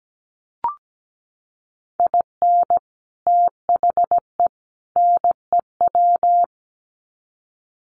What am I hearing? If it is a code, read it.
Morse code, heard as a keyed tone, about 17 wpm, 700 Hz. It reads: IN THE NEW
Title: Sets of 3 Words 17wpm